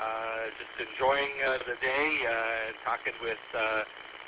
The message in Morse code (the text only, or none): none